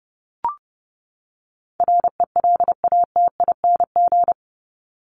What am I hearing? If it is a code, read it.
Morse code, heard as a keyed tone, about 30 wpm, 700 Hz. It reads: RELATING
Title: ING Suffix - Top 100 Words with Base Words 30wpm